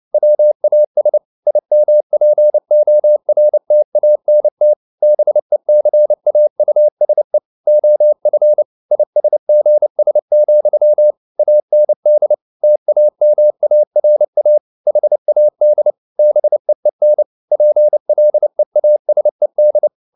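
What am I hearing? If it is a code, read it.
Morse code, heard as a keyed tone, about 29 wpm, 600 Hz. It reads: WAS IMPORTANT BECAUSE OF ISGS, AND TAMARA HAD BEEN PLEASED
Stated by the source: Morse code at 29 wpm, sending Ken Follett's novel Never in English, from CW med Gnister.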